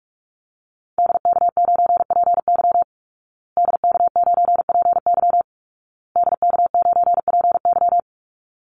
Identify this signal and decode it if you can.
Morse code, heard as a keyed tone, about 44 wpm, 700 Hz. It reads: DK9PY DK9PY DK9PY
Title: Call Signs Repeated 2x - 45wpm